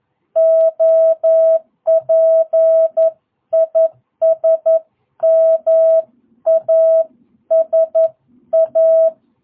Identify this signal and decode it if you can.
Morse code, heard as a keyed tone, about 11 wpm, 650 Hz. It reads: OP IS MASA
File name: jh8cbh / 2022cw